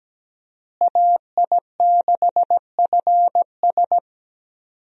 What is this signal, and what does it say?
Morse code, heard as a keyed tone, about 17 wpm, 700 Hz. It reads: AI6FS
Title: USA Call Signs - 5 Characters - Repeated 3x - 17wpm